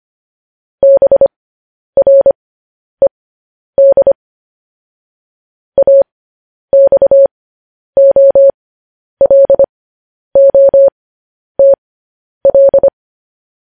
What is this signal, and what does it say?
Morse code, heard as a keyed tone, about 25 wpm, spaced as for 10 wpm, 550 Hz. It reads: BRED AXOLOTL
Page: News Headlines in Morse Code at 10 WPM